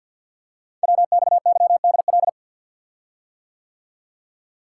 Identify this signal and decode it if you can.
Morse code, heard as a keyed tone, about 52 wpm, 690 Hz. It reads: WXYDL
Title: Code Groups - 5 Letters - 50wpm